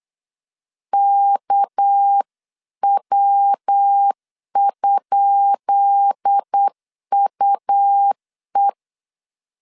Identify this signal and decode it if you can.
Morse code, heard as a keyed tone, about 8 wpm, 800 Hz. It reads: KW?UE
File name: LEZIONI / 10 SIGNS / 10B INT 8wpm 3534